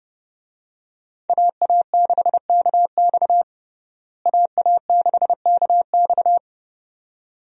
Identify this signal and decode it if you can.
Morse code, heard as a keyed tone, about 30 wpm, 700 Hz. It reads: AA6KX AA6KX
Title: USA Call Signs - 5 Characters - Repeated 2x - 30wpm